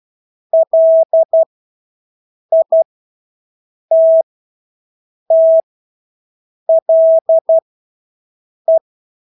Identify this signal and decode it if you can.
Morse code, heard as a keyed tone, about 12 wpm, spaced as for 6 wpm, 650 Hz. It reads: LITTLE
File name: Engelske ord 12-6 wpm